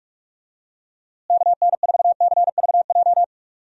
Morse code, heard as a keyed tone, about 41 wpm, 700 Hz. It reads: KN4CVJ
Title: USA Call Signs - 6 Characters - Repeated 2x - 45wpm